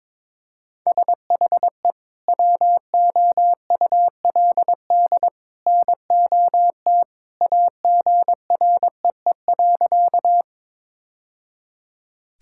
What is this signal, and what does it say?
Morse code, heard as a keyed tone, about 22 wpm, 700 Hz. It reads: SHE WOULD NOT AGREE.
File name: practice / 22wpm / Sentences with 2-Word Phrases - 22wpm